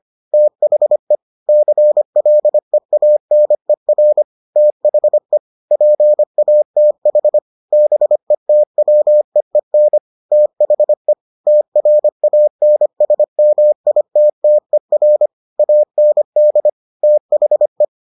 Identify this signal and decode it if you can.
Morse code, heard as a keyed tone, about 25 wpm, 600 Hz. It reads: THE CLEANER THE PATH BETWEEN THE TRANSMITTER AND THE